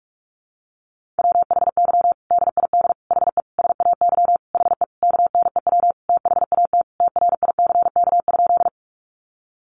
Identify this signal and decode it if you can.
Morse code, heard as a keyed tone, about 44 wpm, 700 Hz. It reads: WHY DID HE SAY HE KNEW THAT TRICK?
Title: Sentences from Top 1500 Words 45wpm